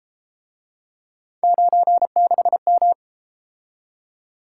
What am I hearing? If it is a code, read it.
Morse code, heard as a keyed tone, about 30 wpm, 700 Hz. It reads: 96M